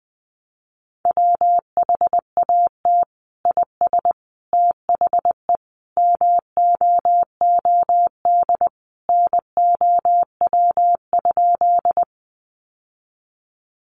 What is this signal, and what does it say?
Morse code, heard as a keyed tone, about 20 wpm, 700 Hz. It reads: WHAT IS THE MOOD NOW?
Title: Sentences from Top 1100 Words 20wpm